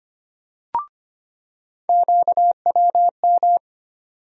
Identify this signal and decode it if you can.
Morse code, heard as a keyed tone, about 25 wpm, 700 Hz. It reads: QWM
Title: USA Call Sign Postfixes - 3 Letters 25wpm